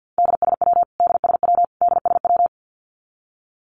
Morse code, heard as a keyed tone, about 49 wpm, 700 Hz. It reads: DSW DSW DSW
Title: Abbreviations Repeated 3x - 50wpm